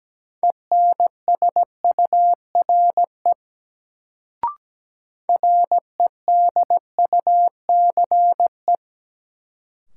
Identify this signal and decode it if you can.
Morse code, heard as a keyed tone, about 17 wpm, 700 Hz. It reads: ENSURE REDUCE
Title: Sentences from Top 1100 Words - New Word Drill Alt - 17wpm